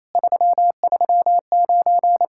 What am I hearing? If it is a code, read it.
Morse code, heard as a keyed tone, about 37 wpm, 700 Hz. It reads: 339